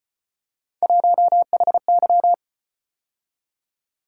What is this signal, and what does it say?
Morse code, heard as a keyed tone, about 33 wpm, 700 Hz. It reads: 1HY